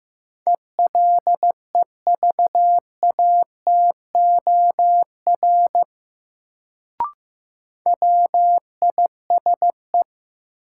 Morse code, heard as a keyed tone, about 15 wpm, 700 Hz. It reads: ELEVATOR WISE
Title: Sentences from Top 1600 Words - New Word Drill Alt - 15wpm